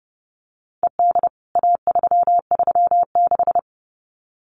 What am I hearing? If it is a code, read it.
Morse code, heard as a keyed tone, about 30 wpm, 700 Hz. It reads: ED A336
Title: CWT Exchange 30wpm